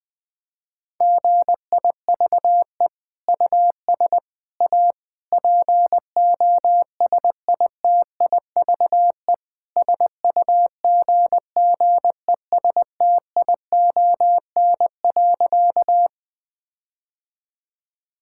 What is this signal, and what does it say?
Morse code, heard as a keyed tone, about 20 wpm, 700 Hz. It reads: GIVE US A POSITIVE SUGGESTION.
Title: Sentences from Top 1600 Words 20wpm